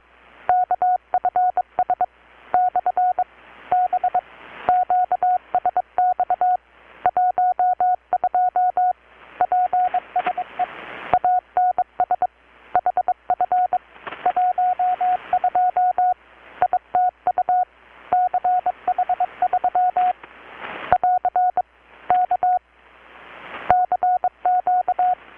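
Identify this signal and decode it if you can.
Morse code, heard as a keyed tone, about 22 wpm, 720 Hz. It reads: KFS / B QSX 12 PSE ANS HF 12 ITU CH3 <AR> K CQ
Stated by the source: Morse from KPH's Night of Nights XX maritime radiotelegraphy event, received in Chiapas, Mexico about 3,800 km away on a Kenwood R-600 with a grounded 10 m long-wire antenna.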